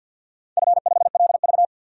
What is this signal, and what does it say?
Morse code, heard as a keyed tone, about 52 wpm, 700 Hz. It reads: F5LV
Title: DXCC-100 Call Signs - 4 Characters - 50wpm